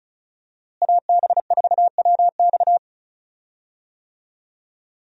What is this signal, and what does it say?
Morse code, heard as a keyed tone, about 36 wpm, 700 Hz. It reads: AB4WX
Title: USA Call Signs - 5 Characters - Repeated 3x - 35wpm